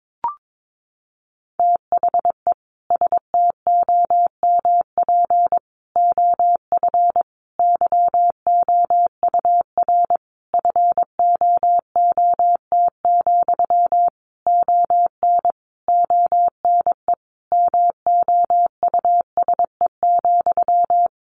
Morse code, heard as a keyed tone, about 22 wpm, 700 Hz. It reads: THE STOMP OF YOUR FOOT, ON ONE MOUSE,
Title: A Sound Of Thunder 22wpm